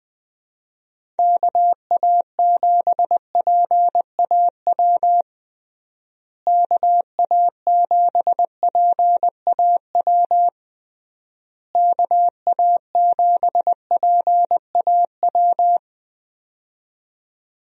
Morse code, heard as a keyed tone, about 21 wpm, 700 Hz. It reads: KA7PAW KA7PAW KA7PAW
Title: USA Call Signs - 6 Characters - Repeated 3x - 20wpm